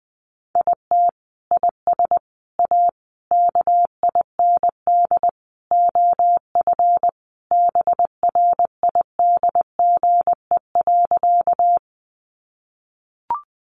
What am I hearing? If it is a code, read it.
Morse code, heard as a keyed tone, about 20 wpm, 700 Hz. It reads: IT IS A KIND OF BRIDGE.
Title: Sentences from Top 1200 Words 20wpm